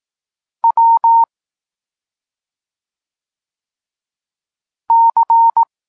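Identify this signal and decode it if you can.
Morse code, heard as a keyed tone, about 18 wpm, 920 Hz. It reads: W C
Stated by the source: Polish Morse lesson teaching the letter C, from SP9MOA Niepołomicki Klub Krótkofalowców.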